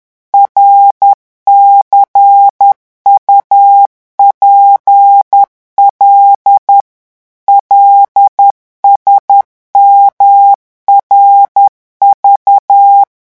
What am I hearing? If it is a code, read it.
Morse code, heard as a keyed tone, about 10 wpm, 800 Hz. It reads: RCUPL LSMRV
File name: morse mp3 / revision5-vitesse 10 mots